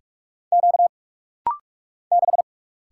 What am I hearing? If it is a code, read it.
Morse code, heard as a keyed tone, about 46 wpm, 700 Hz. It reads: Q 6
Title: Single Letter-Number Alt - Rapid-Fire 45wpm